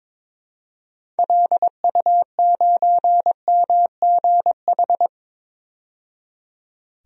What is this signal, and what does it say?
Morse code, heard as a keyed tone, about 22 wpm, 700 Hz. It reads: LU9MGH
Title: DXCC-100 Call Signs - 6 Characters - 22wpm